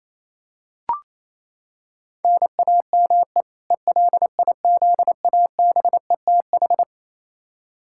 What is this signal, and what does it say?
Morse code, heard as a keyed tone, about 28 wpm, 700 Hz. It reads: NAME ELIZABETH
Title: K1USN SST QSO Practice 28wpm